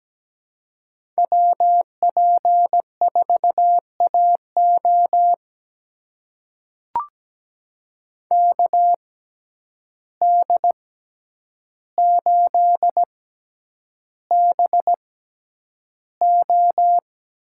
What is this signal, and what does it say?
Morse code, heard as a keyed tone, about 17 wpm, spaced as for 9 wpm, 700 Hz. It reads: WP4AO KD8BO
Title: USA Call Signs - 5 Characters - 6x Character Spacing - 17wpm